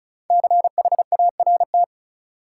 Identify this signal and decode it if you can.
Morse code, heard as a keyed tone, about 34 wpm, 700 Hz. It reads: CHART
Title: Sentences from Top 1300 Words - New Word Drill - 35wpm